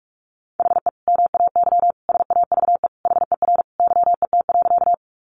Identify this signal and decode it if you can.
Morse code, heard as a keyed tone, about 44 wpm, 700 Hz. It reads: HE MAY SAVE HER YET.
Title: Sentences from Top 1100 Words 45wpm